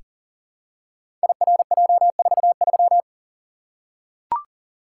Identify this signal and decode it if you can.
Morse code, heard as a keyed tone, about 42 wpm, 700 Hz. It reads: IRJ43
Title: Code Groups - 5 Letter:Numbers - 40wpm